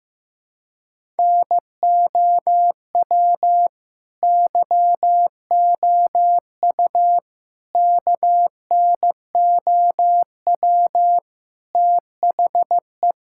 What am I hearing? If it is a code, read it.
Morse code, heard as a keyed tone, about 15 wpm, 700 Hz. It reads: NOW YOU KNOW THE
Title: Sentences from Top 1000 Words - Review Part 6 - 15wpm